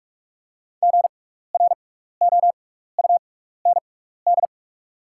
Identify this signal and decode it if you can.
Morse code, heard as a keyed tone, about 46 wpm, spaced as for 16 wpm, 700 Hz. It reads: GROUND
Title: 6x Character Spacing - Top 500 Words - 45wpm